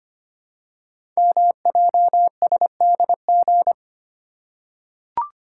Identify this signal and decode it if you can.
Morse code, heard as a keyed tone, about 25 wpm, 700 Hz. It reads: MJSDG
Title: Code Groups - 5 Letters - 25wpm